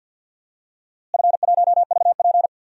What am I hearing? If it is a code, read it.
Morse code, heard as a keyed tone, about 47 wpm, 700 Hz. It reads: F1VP
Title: DXCC-100 Call Signs - 4 Characters - Repeated 2x - 50wpm